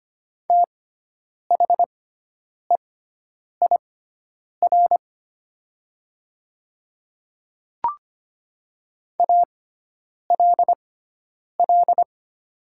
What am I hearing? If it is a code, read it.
Morse code, heard as a keyed tone, about 25 wpm, spaced as for 9 wpm, 700 Hz. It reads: THEIR ALL
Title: 6x Character Spacing - Top 100 Words - 25wpm